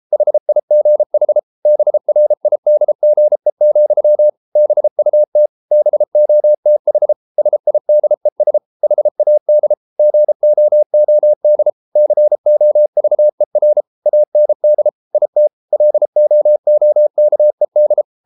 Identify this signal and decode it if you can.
Morse code, heard as a keyed tone, about 33 wpm, 600 Hz. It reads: HIGH BRIDGE, BUT BOTH SIDES HAD GOOD COVER AND IT LOOKED